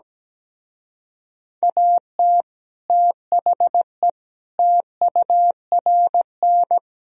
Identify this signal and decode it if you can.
Morse code, heard as a keyed tone, about 17 wpm, 700 Hz. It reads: AT THE TURN
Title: Speed-Racing - Sets of 5 Words 17wpm